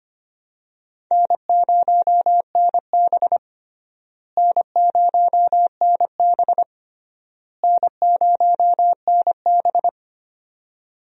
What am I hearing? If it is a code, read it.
Morse code, heard as a keyed tone, about 25 wpm, 700 Hz. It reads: N0NB N0NB N0NB
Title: USA Call Signs - 4 Characters - Repeated 3x - 25wpm